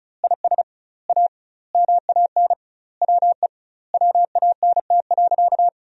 Morse code, heard as a keyed tone, about 36 wpm, 700 Hz. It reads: IS A MAN WE WANT.
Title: 2x Word-Spacing - Sentences from Top 200 Words - 35wpm